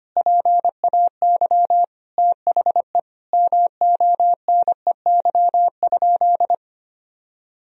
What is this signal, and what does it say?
Morse code, heard as a keyed tone, about 25 wpm, 700 Hz. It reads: PAY THE MONEY?
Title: Sentences from Top 1000 Words - Review Part 2 - 25wpm